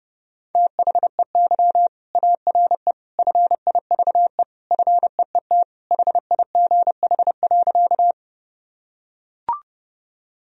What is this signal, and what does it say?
Morse code, heard as a keyed tone, about 30 wpm, 700 Hz. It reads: THEY ARE FIVE FEET HIGH.
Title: Sentences from Top 1000 Words - Review Part 2 - 30wpm